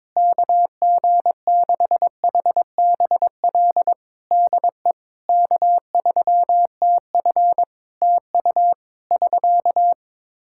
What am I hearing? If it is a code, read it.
Morse code, heard as a keyed tone, about 22 wpm, 700 Hz. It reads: KG6HBL DE K3TF TU <SK>
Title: Practice QSO - Exchange 4 - Ending - 22wpm